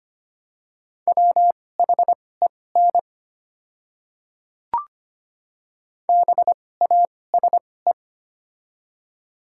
Morse code, heard as a keyed tone, about 25 wpm, spaced as for 18 wpm, 700 Hz. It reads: WHEN BASE
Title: 2x Character Spacing - Top 500 Words - 25wpm